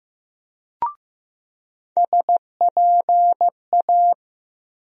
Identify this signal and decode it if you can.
Morse code, heard as a keyed tone, about 15 wpm, 700 Hz. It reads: SPA
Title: USA Call Sign Postfixes - 3 Letters 15wpm